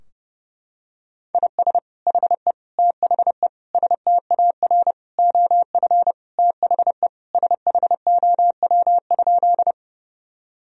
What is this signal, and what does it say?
Morse code, heard as a keyed tone, about 30 wpm, 700 Hz. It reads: IS HE THE STAR OF THE SHOW?
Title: Sentences from Top 1000 Words - Review Part 2 - 30wpm